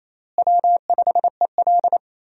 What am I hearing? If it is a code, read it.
Morse code, heard as a keyed tone, about 28 wpm, 700 Hz. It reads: W5EL
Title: USA Call Signs - 4 Characters - Repeated 3x - 28wpm